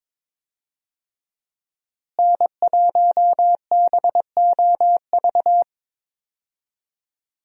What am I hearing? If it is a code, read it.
Morse code, heard as a keyed tone, about 22 wpm, 700 Hz. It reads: N1BOV
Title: USA Call Signs - 5 Characters - Repeated 2x - 22wpm